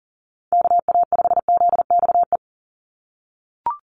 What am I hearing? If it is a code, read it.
Morse code, heard as a keyed tone, about 40 wpm, 700 Hz. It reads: KA5ZXE
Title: USA Call Signs 40wpm